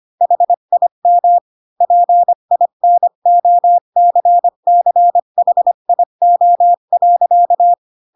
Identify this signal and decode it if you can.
Morse code, heard as a keyed tone, about 25 wpm, 700 Hz. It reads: HIM PINOCCHIO.
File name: Pinocchio / Pinocchio ch3 20 25 ik1hge